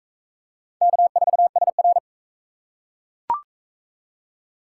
Morse code, heard as a keyed tone, about 41 wpm, 700 Hz. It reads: K4SR